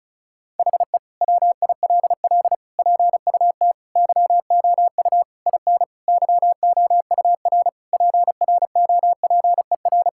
Morse code, heard as a keyed tone, about 34 wpm, 700 Hz. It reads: HE WILL PUT YOU IN YOUR PROPER
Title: Sentences from Top 1000 Words - Review Part 5 - 35wpm